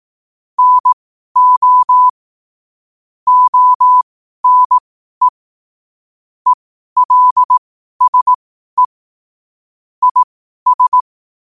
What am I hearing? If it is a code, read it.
Morse code, heard as a keyed tone, about 18 wpm, spaced as for 12 wpm, 1000 Hz. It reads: NO ONE ELSE IS
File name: qotdfiles / 12wpm